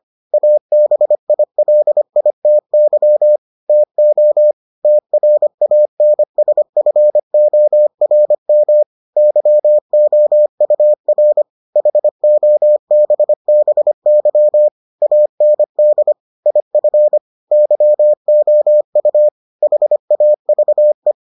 Morse code, heard as a keyed tone, about 25 wpm, 600 Hz. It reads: ABILITY TO TRANSFORM YOUR HOBBY AND IF YOU HAVE